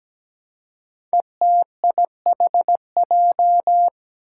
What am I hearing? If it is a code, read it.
Morse code, heard as a keyed tone, about 17 wpm, 700 Hz. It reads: ETIHJ
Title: Five Challenging Letter-Number Combinations 17wpm